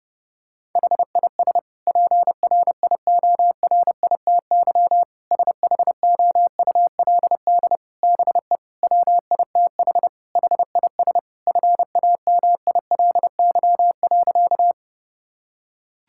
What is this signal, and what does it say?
Morse code, heard as a keyed tone, about 32 wpm, 700 Hz. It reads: HIS PRIORITY SHOULD BE WITH HIS FAMILY.
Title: Sentences from Top 1600 Words 30wpm